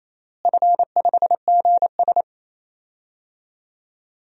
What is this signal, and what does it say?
Morse code, heard as a keyed tone, about 28 wpm, 700 Hz. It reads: F5GS
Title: DXCC-100 Call Signs - 4 Characters - Repeated 2x - 28wpm